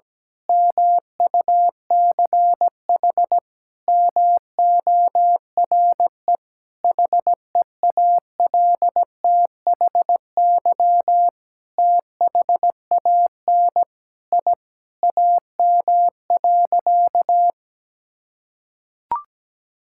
Morse code, heard as a keyed tone, about 17 wpm, 700 Hz. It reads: MUCH MORE HEALTHY THAN I AM.
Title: Sentences from Top 1000 Words - Review Part 5 - 17wpm